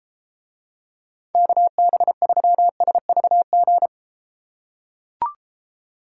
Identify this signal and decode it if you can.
Morse code, heard as a keyed tone, about 31 wpm, 700 Hz. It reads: KB3SVG